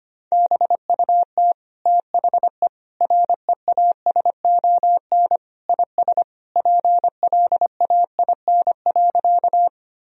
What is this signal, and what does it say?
Morse code, heard as a keyed tone, about 25 wpm, 700 Hz. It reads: BUT THE REASON IS PLAIN.
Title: Sentences from Top 1000 Words - Review Part 2 - 25wpm